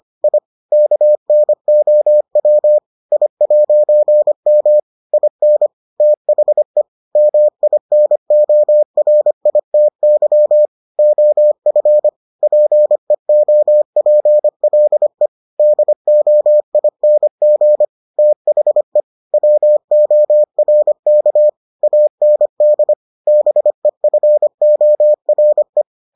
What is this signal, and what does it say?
Morse code, heard as a keyed tone, about 25 wpm, 600 Hz. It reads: I KNOW I'M IN THE MINORITY OF PEOPLE DOING THE WORK AND BEFORE